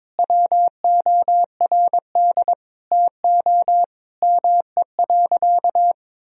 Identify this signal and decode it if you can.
Morse code, heard as a keyed tone, about 21 wpm, 700 Hz. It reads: WORD TO ME.
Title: Sentences from Top 1000 Words - Review Part 4 - 22wpm